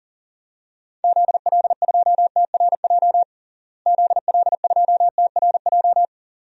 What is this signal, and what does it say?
Morse code, heard as a keyed tone, about 40 wpm, 700 Hz. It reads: ZL2TRJ ZL2TRJ
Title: DXCC-100 Call Signs - 6 Characters - Repeated 2x - 40wpm